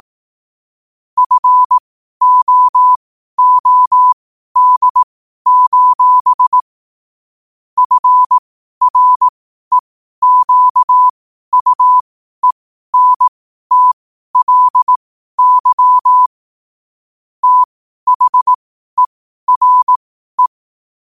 Morse code, heard as a keyed tone, about 18 wpm, spaced as for 12 wpm, 1000 Hz. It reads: FOOD: FREQUENTLY THERE
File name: qotdfiles / 12wpm